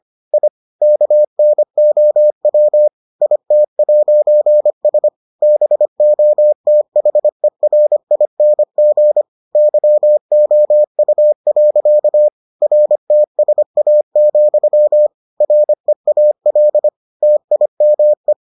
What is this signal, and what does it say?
Morse code, heard as a keyed tone, about 25 wpm, 600 Hz. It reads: I KNOW IT'S BOTHERING YOU. RTSA, REAL TIME